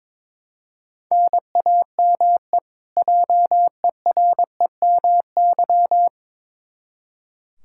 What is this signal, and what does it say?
Morse code, heard as a keyed tone, about 22 wpm, 700 Hz. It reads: NAME JEREMY
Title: K1USN SST QSO Practice 22wpm